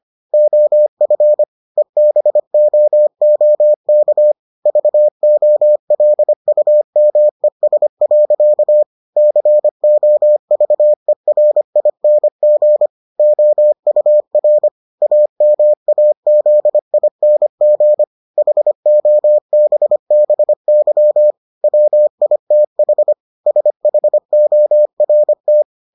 Morse code, heard as a keyed tone, about 25 wpm, 600 Hz. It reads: OF EBOOK VOLUMES. COVERING OUR AMAZING HOBBY WITH SHORT